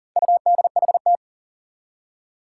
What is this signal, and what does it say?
Morse code, heard as a keyed tone, about 38 wpm, 700 Hz. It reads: UDHT
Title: Four Challenging Letter-Number Combinations 40wpm